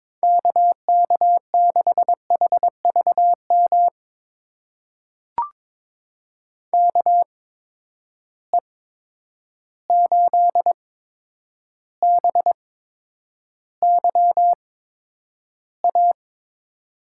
Morse code, heard as a keyed tone, about 22 wpm, spaced as for 9 wpm, 700 Hz. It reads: KK6HVM KE8BYA
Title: USA Call Signs - 6 Characters - 8x Character Spacing - 22wpm